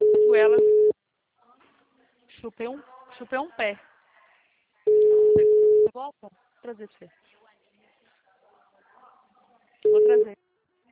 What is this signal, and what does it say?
Ringback tone (Europe, 425 Hz), with speech.